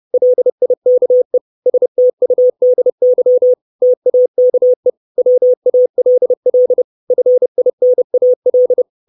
Morse code, heard as a keyed tone, about 30 wpm, 500 Hz. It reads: LIKE STUDY TAKE WALL FINAL
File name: 632 Words 30WPM